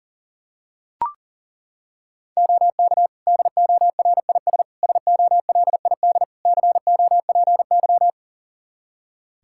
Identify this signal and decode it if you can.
Morse code, heard as a keyed tone, about 40 wpm, 700 Hz. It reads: OK DORIS SOLID COPY
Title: Practice QSO - Exchange 3 - More Info - 40wpm